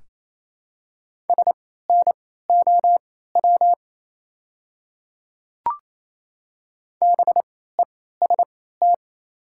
Morse code, heard as a keyed tone, about 28 wpm, spaced as for 16 wpm, 700 Hz. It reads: SNOW BEST
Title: 3x Character Spacing - Top 500 Words - 28wpm